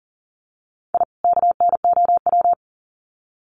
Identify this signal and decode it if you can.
Morse code, heard as a keyed tone, about 40 wpm, 700 Hz. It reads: I KNOW
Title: Sets of 2 Words 40wpm